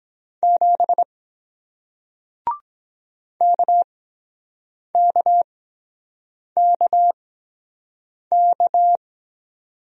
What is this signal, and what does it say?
Morse code, heard as a keyed tone, about 24 wpm, spaced as for 7 wpm, 700 Hz. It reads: 7 KKKK